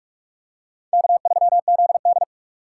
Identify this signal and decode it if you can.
Morse code, heard as a keyed tone, about 40 wpm, 700 Hz. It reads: K3ZD